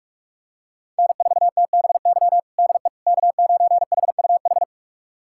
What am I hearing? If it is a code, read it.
Morse code, heard as a keyed tone, about 43 wpm, 700 Hz. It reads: N4TBY DE K9HUH
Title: USA Callsign DE Callsign - 5 Characters - 45wpm